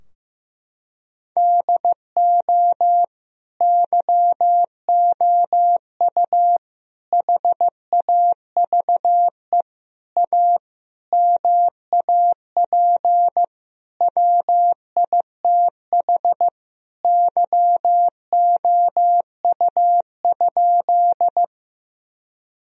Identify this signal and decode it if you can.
Morse code, heard as a keyed tone, about 15 wpm, 700 Hz. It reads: DO YOU HAVE A MAP WITH YOU?
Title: Sentences from Top 1000 Words - Review Part 2 - 15wpm